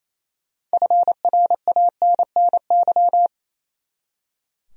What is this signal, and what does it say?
Morse code, heard as a keyed tone, about 28 wpm, 700 Hz. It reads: FRANNY
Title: USA Common Names - 28wpm